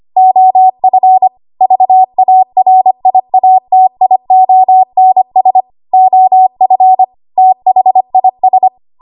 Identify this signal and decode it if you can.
Morse code, heard as a keyed tone, about 25 wpm, 750 Hz. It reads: OF VARIATIONS OF THIS
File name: Morse / 25WPM